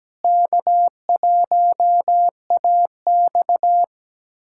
Morse code, heard as a keyed tone, about 17 wpm, 700 Hz. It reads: K1AX
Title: USA Callsign DE Callsign - 4 Characters - 17wpm